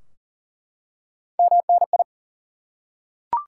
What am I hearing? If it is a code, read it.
Morse code, heard as a keyed tone, about 40 wpm, 700 Hz. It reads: MNI
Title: Top 90 QSO Elements 40wpm